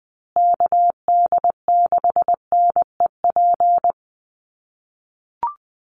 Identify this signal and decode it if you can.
Morse code, heard as a keyed tone, about 20 wpm, 700 Hz. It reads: KD6NEP
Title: USA Call Signs Repeated 3x 20wpm